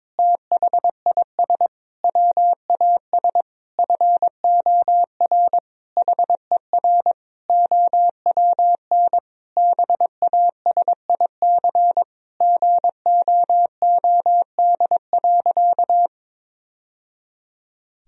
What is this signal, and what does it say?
Morse code, heard as a keyed tone, about 22 wpm, 700 Hz. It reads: THIS WAS FOR HER OWN BASIC GOOD.
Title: Sentences from Top 1000 Words - Review Part 3 - 22wpm